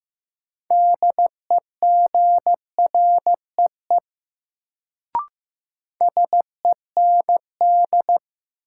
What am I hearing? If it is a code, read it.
Morse code, heard as a keyed tone, about 15 wpm, 700 Hz. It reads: DEGREE SEND
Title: Sentences from Top 1500 Words - New Word Drill Alt - 15wpm